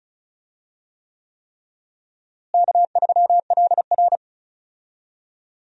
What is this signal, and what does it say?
Morse code, heard as a keyed tone, about 36 wpm, 700 Hz. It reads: K3LR
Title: USA Call Signs - 4 Characters - Repeated 3x - 35wpm